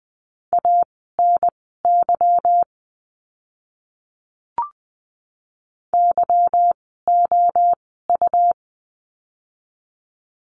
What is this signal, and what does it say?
Morse code, heard as a keyed tone, about 20 wpm, spaced as for 14 wpm, 700 Hz. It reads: ANY YOU
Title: 192 Lesson 31f - Letter v - Words 20wpm